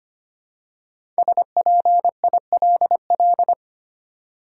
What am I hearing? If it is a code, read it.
Morse code, heard as a keyed tone, about 25 wpm, 700 Hz. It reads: SPILL
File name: 5 Letter Words - First 3 Letters Given - 25wpm